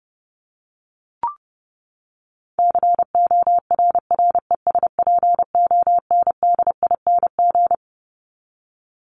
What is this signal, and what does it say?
Morse code, heard as a keyed tone, about 30 wpm, 700 Hz. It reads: CORRESPONDING
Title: ING Suffix - Top 100 Words with Base Words 30wpm